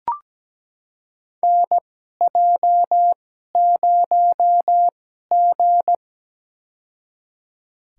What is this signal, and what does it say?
Morse code, heard as a keyed tone, about 17 wpm, spaced as for 12 wpm, 700 Hz. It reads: NJ0G
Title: USA Call Signs - 2x Character Spacing - 17wpm